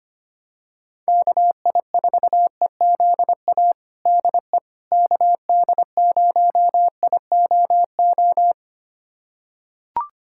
Morse code, heard as a keyed tone, about 25 wpm, 700 Hz. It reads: KI4EZA DE KD0IOO
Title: USA Callsign DE Callsign - 6 Characters - 25wpm